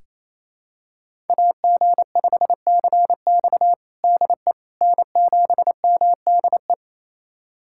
Morse code, heard as a keyed tone, about 28 wpm, 700 Hz. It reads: AG5CX DE N7MDE
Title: USA Callsign DE Callsign - 5 Characters - 28wpm